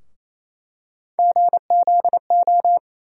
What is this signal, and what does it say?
Morse code, heard as a keyed tone, about 28 wpm, 700 Hz. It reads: GZO
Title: USA Call Sign Postfixes - 3 Letters 28wpm